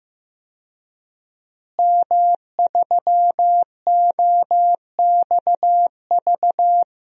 Morse code, heard as a keyed tone, about 15 wpm, 700 Hz. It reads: M3OXV
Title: DXCC-100 Call Signs - 5 Characters - Repeated 3x - 15wpm